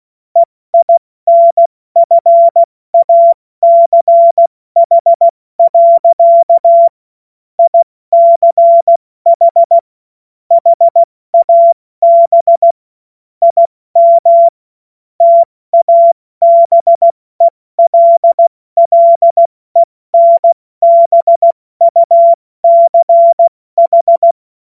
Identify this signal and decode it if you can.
Morse code, heard as a keyed tone, about 16 wpm, 680 Hz. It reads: EINFACH. ICH HAB IM TABELLENBUCH
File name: Morsekurs / Witze / Witz 21. Dezember 2021, 70 F80